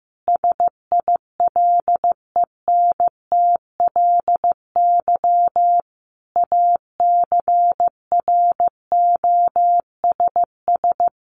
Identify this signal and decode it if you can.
Morse code, heard as a keyed tone, about 15 wpm, 700 Hz. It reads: SILENTLY ACROSS
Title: A Sound Of Thunder 15wpm